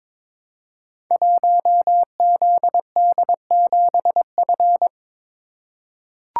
Morse code, heard as a keyed tone, about 22 wpm, 700 Hz. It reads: 1ZD7F
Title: Code Groups - 5 Letter:Numbers - 22wpm